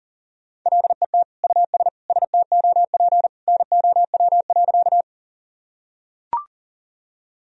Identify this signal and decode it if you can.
Morse code, heard as a keyed tone, about 40 wpm, 700 Hz. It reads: LET US STOP NOW.
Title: Sentences from Top 1000 Words - Review Part 1 - 40wpm